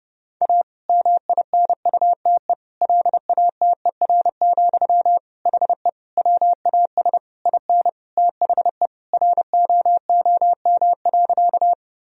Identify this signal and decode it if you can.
Morse code, heard as a keyed tone, about 30 wpm, 700 Hz. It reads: A MINUTE LATER, HE WAS IN THE ROOM.
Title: Sentences from Top 1000 Words - Review Part 4 - 30wpm